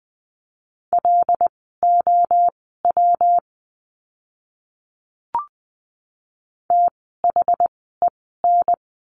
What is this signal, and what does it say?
Morse code, heard as a keyed tone, about 20 wpm, spaced as for 14 wpm, 700 Hz. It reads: LOW THEN
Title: 115 Lesson 19f - Letter w - Words 20wpm